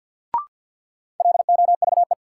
Morse code, heard as a keyed tone, about 53 wpm, 700 Hz. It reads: LOVE
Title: Sentences from Top 400 Words - New Word Drill - 50wpm